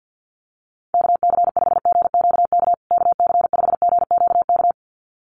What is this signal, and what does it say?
Morse code, heard as a keyed tone, about 50 wpm, 700 Hz. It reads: KC5GQK KC5GQK
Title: USA Call Signs Repeated 2x 50wpm